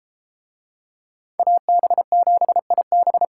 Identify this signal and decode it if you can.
Morse code, heard as a keyed tone, about 32 wpm, 700 Hz. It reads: AB7IB